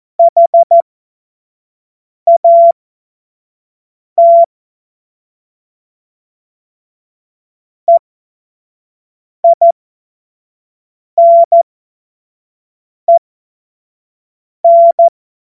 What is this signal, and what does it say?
Morse code, heard as a keyed tone, about 14 wpm, spaced as for 5 wpm, 680 Hz. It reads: HAT EINEN
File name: Morsekurs / Witze / Witz 05. April 2022 25 F70